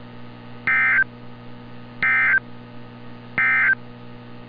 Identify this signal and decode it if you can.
Emergency Alert System (SAME) tones, heard as harsh buzzing bursts. It is heard from 0.6 to 3.8 s.
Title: NOAA Weather Radio - Altoona, PA